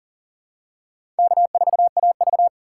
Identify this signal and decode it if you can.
Morse code, heard as a keyed tone, about 40 wpm, 700 Hz. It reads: K4AV
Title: USA Call Signs - 4 Characters - 3x Character Spacing - 40wpm